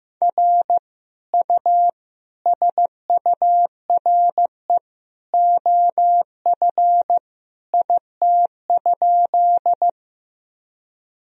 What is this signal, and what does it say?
Morse code, heard as a keyed tone, about 16 wpm, 700 Hz. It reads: R U SURE OF IT?
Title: Sentences from Top 300 Words and Abbreviations 15wpm